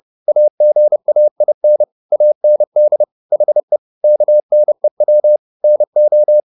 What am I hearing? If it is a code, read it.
Morse code, heard as a keyed tone, about 30 wpm, 600 Hz. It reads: AGAIN AND HE KNEW NO
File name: the old man and the sea / the old man and the sea 1 30wpm